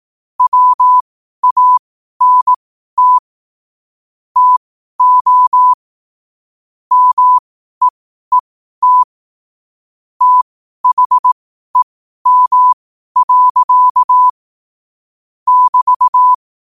Morse code, heard as a keyed tone, about 18 wpm, spaced as for 12 wpm, 1000 Hz. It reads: WANT TO MEET THEM. <BT>
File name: qotdfiles / 12wpm